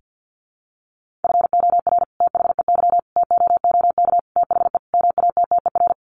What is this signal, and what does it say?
Morse code, heard as a keyed tone, about 50 wpm, 700 Hz. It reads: FOR THEY TOOK THE MATTER
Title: Sentences from Top 1400 Words 50wpm